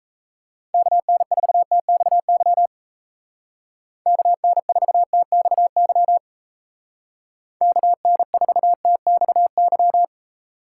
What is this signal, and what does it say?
Morse code, heard as a keyed tone, about 34 wpm, 700 Hz. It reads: KN4TXY KN4TXY KN4TXY
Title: Speed Racing - USA 2x3 Call Signs 28wpm